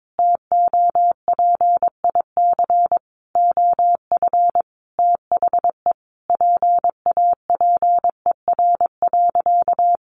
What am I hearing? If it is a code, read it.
Morse code, heard as a keyed tone, about 22 wpm, 700 Hz. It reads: TOPIC OF THE PAPER.
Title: Sentences from Top 1200 Words 22wpm